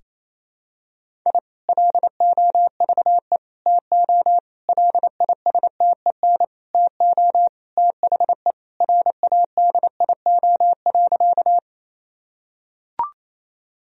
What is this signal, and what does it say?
Morse code, heard as a keyed tone, about 28 wpm, 700 Hz. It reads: I LOVE TO LISTEN TO THE RADIO.
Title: Sentences from Top 1000 Words - Review Part 3 - 28wpm